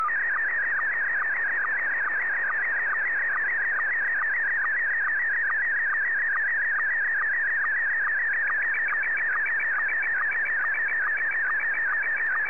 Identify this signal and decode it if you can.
SSTV (slow-scan television), Scottie 1.